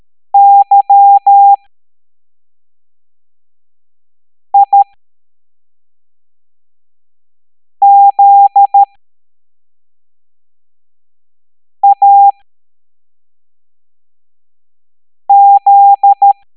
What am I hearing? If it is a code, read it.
Morse code, heard as a keyed tone, about 13 wpm, spaced as for 3 wpm, 800 Hz. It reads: YIZAZ
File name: hamdata / L09-EX04